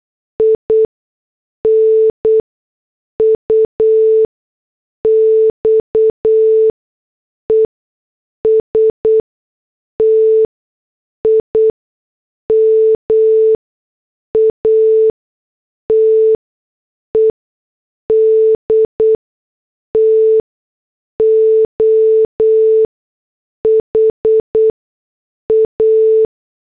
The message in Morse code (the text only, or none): INUX ESTIMATED TO HA